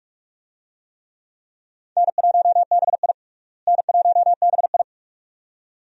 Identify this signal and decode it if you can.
Morse code, heard as a keyed tone, about 47 wpm, 700 Hz. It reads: N1BI N1BI
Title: USA Call Signs - 4 Characters - Repeated 2x - 45wpm